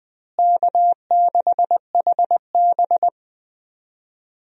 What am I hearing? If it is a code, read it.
Morse code, heard as a keyed tone, about 21 wpm, 700 Hz. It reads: K6HB
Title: USA Call Signs - 4 Characters - Repeated 2x - 20wpm